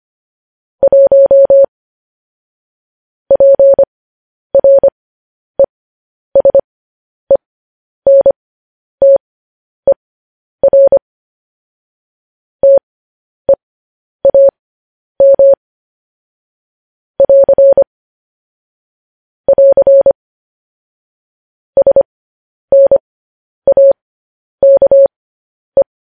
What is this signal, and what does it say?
Morse code, heard as a keyed tone, about 25 wpm, spaced as for 10 wpm, 550 Hz. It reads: 1 PRESENTER TEAM <AR> <AR> SNAKE
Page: news-headlines-in-morse-code-at-10-wpm